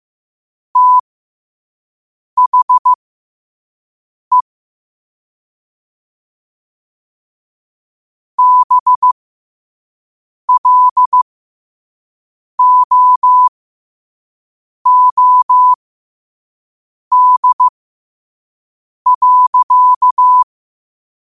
Morse code, heard as a keyed tone, about 15 wpm, spaced as for 5 wpm, 1000 Hz. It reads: THE BLOOD.
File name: qotdfiles / 5wpm